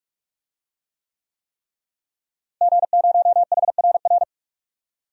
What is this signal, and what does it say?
Morse code, heard as a keyed tone, about 46 wpm, 700 Hz. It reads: G0HRR
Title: DXCC-100 Call Signs - 5 Characters - Repeated 3x - 45wpm